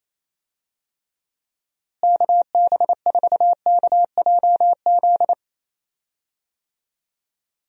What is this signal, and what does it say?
Morse code, heard as a keyed tone, about 28 wpm, 700 Hz. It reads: KB4KJZ
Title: USA Call Signs - 6 Characters - Repeated 3x - 28wpm